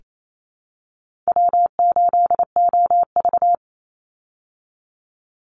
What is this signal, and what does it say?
Morse code, heard as a keyed tone, about 28 wpm, 700 Hz. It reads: W8OV
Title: Easy Call Signs Repeated 2x - 28wpm